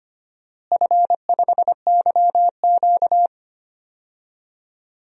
Morse code, heard as a keyed tone, about 25 wpm, 700 Hz. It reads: F5YQ
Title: DXCC-100 Call Signs - 4 Characters - Repeated 3x - 25wpm